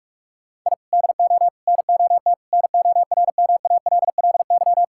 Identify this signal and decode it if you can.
Morse code, heard as a keyed tone, about 46 wpm, 700 Hz. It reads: I DO NOT NORMALLY
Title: Sentences from Top 1000 Words - Review Part 6 - 45wpm